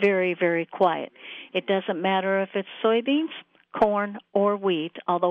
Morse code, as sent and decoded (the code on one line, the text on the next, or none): none